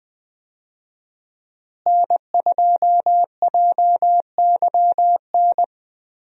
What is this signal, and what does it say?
Morse code, heard as a keyed tone, about 20 wpm, 700 Hz. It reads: N2JYN
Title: USA Call Signs - 5 Characters - Repeated 3x - 20wpm